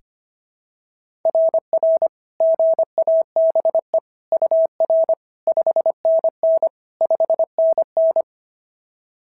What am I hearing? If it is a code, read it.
Morse code, heard as a keyed tone, about 22 wpm, 650 Hz. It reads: RR GABE UR 5NN 5NN
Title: SOTA QSO Practice 25wpm